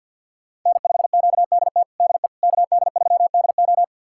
Morse code, heard as a keyed tone, about 51 wpm, 700 Hz. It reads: N5QDT DE KD3DO
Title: USA Callsign DE Callsign - 5 Characters - 50wpm